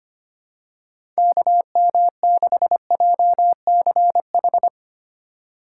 Morse code, heard as a keyed tone, about 25 wpm, 700 Hz. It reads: KM6JCH
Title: USA Call Signs - 6 Characters - 3x Character Spacing - 25wpm